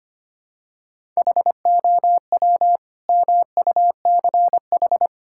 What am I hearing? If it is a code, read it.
Morse code, heard as a keyed tone, about 25 wpm, 700 Hz. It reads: HOW MUCH
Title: Sentences from Top 1000 Words - Review Part 6 - 25wpm